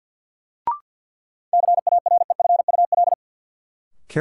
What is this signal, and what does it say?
Morse code, heard as a keyed tone, about 52 wpm, 700 Hz. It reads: CAREFUL
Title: Sentences from Top 1600 Words - New Word Drill Alt - 50wpm